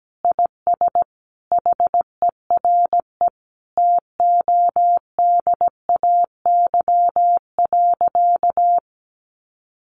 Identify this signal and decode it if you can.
Morse code, heard as a keyed tone, about 17 wpm, 700 Hz. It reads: IS HERE TODAY.
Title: Sentences from Top 1100 Words 17wpm